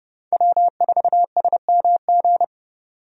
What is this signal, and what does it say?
Morse code, heard as a keyed tone, about 30 wpm, 700 Hz. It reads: W4SMG
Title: USA Call Signs - 5 Characters - 30wpm